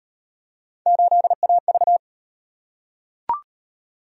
Morse code, heard as a keyed tone, about 36 wpm, 700 Hz. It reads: 8AV